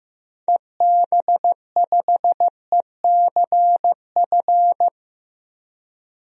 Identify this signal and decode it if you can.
Morse code, heard as a keyed tone, about 15 wpm, 700 Hz. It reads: EB5ECF
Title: DXCC-100 Call Signs - 6 Characters - Repeated 2x - 15wpm